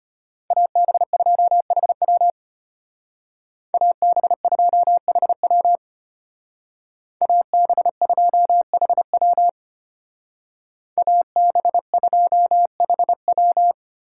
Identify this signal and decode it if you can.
Morse code, heard as a keyed tone, about 32 wpm, 700 Hz. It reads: AB2HW AB2HW AB2HW AB2HW